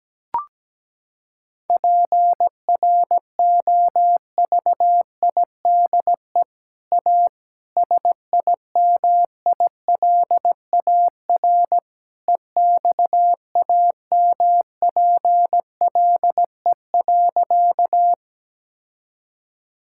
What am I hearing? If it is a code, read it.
Morse code, heard as a keyed tone, about 17 wpm, 700 Hz. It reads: PROVIDE A SIMILAR EXAMPLE.
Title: Sentences from Top 1000 Words - Review Part 5 - 17wpm